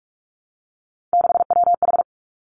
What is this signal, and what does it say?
Morse code, heard as a keyed tone, about 46 wpm, 700 Hz. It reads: 6WH
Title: Three Challenging Letter-Number Combinations 45wpm